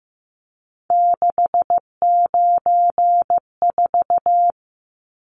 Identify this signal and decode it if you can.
Morse code, heard as a keyed tone, about 15 wpm, 700 Hz. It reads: 694
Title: Numbers - 3-Digit 15wpm